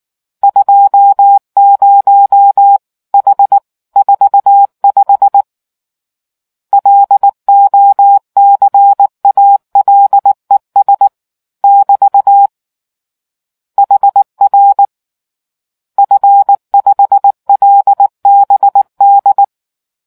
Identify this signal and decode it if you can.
Morse code, heard as a keyed tone, about 19 wpm, 800 Hz. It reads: 20H 45 LOCALES <BT> HR F5LBD